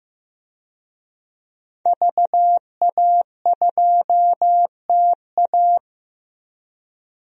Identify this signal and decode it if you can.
Morse code, heard as a keyed tone, about 15 wpm, 700 Hz. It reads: VA2TA
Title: DXCC-100 Call Signs - 5 Characters - 15wpm